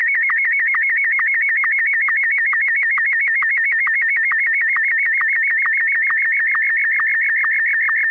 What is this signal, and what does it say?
SSTV (slow-scan television), Martin 1.